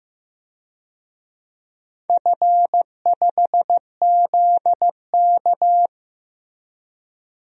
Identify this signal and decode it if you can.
Morse code, heard as a keyed tone, about 15 wpm, 700 Hz. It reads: F5ZK
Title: DXCC-100 Call Signs - 4 Characters - Repeated 2x - 15wpm